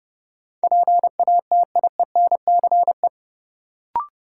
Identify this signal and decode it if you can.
Morse code, heard as a keyed tone, about 30 wpm, 700 Hz. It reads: PATIENCE
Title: Sentences from Top 1300 Words - New Word Drill Alt - 30wpm